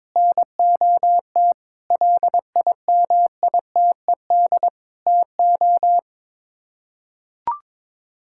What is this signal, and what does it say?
Morse code, heard as a keyed tone, about 22 wpm, 700 Hz. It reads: NOT LIMITED TO
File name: practice / 22wpm / Sets of 6 Words - 22wpm